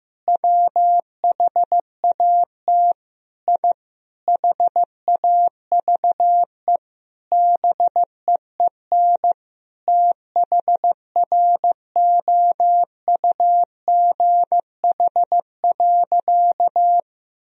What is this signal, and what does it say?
Morse code, heard as a keyed tone, about 15 wpm, 700 Hz. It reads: WHAT I HAVE BEEN THROUGH.
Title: Sentences from Top 1000 Words - Review Part 1 - 15wpm